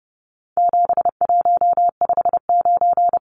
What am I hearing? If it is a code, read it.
Morse code, heard as a keyed tone, about 30 wpm, 700 Hz. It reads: 7159
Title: Numbers - 4-Digit 30wpm